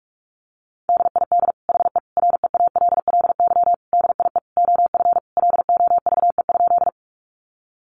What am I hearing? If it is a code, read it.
Morse code, heard as a keyed tone, about 46 wpm, 700 Hz. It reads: DID HE REALLY DIE OF LOVE?
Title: Sentences from Top 1300 Words 45wpm